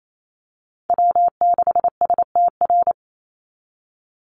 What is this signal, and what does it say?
Morse code, heard as a keyed tone, about 28 wpm, 700 Hz. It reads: W6STR
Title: Call Signs Repeated 3x - 28wpm